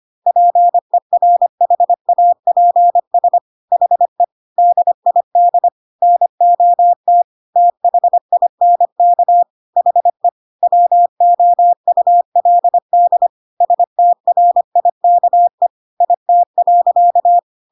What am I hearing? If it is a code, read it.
Morse code, heard as a keyed tone, about 25 wpm, 700 Hz. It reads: PERHAPS HE DID NOT THINK HE WOULD STRIKE IT.
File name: Pinocchio / Pinocchio ch4 25 25 ik1hge